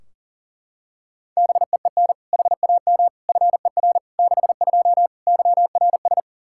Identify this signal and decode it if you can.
Morse code, heard as a keyed tone, about 40 wpm, 700 Hz. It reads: BEEN HAM FER 62 YRS
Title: K1USN SST QSO Practice 40wpm